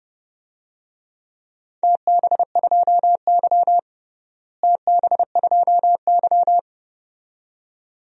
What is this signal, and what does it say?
Morse code, heard as a keyed tone, about 30 wpm, 700 Hz. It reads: TB2Y TB2Y
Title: DXCC-100 Call Signs - 4 Characters - Repeated 2x - 30wpm